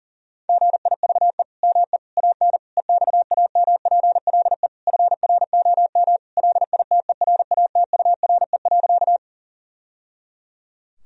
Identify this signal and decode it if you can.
Morse code, heard as a keyed tone, about 40 wpm, 700 Hz. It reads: GIVE ME AN EXAMPLE FROM LITERATURE.
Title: Sentences from Top 1000 Words - Review Part 3 - 40wpm